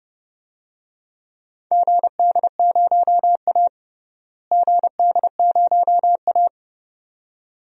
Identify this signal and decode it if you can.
Morse code, heard as a keyed tone, about 30 wpm, 700 Hz. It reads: GD0A GD0A
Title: DXCC-100 Call Signs - 4 Characters - Repeated 2x - 30wpm